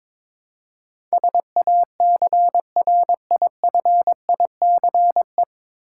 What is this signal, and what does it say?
Morse code, heard as a keyed tone, about 22 wpm, 700 Hz. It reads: SACRIFICE
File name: 9 Letter Words - First 3 Letters Given - 22wpm